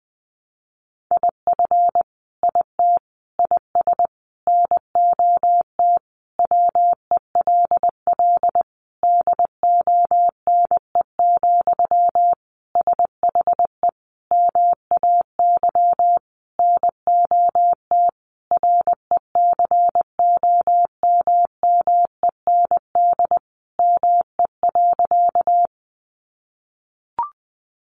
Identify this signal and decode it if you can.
Morse code, heard as a keyed tone, about 19 wpm, 700 Hz. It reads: IF IT IS NOT WELL DONE, SHE MAY NOT RECOMMEND ME.
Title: Sentences from Top 1200 Words 20wpm